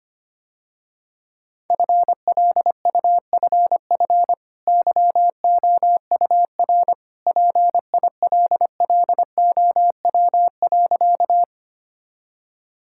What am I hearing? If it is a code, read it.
Morse code, heard as a keyed tone, about 25 wpm, 700 Hz. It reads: FLUFF YOUR PILLOW.
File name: Easy Sentences with Hints - 25wpm